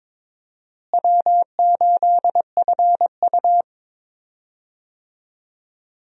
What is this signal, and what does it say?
Morse code, heard as a keyed tone, about 22 wpm, 700 Hz. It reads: W8FU
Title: USA Call Signs - 4 Characters - Repeated 2x - 22wpm